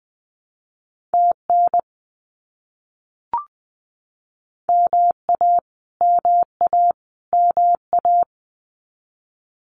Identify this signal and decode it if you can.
Morse code, heard as a keyed tone, about 20 wpm, 700 Hz. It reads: TN MA MA MA
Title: US States Repeated 3x 20wpm